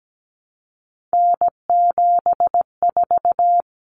Morse code, heard as a keyed tone, about 17 wpm, 700 Hz. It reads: N74
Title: Common Cut Numbers - 3-Digit 17wpm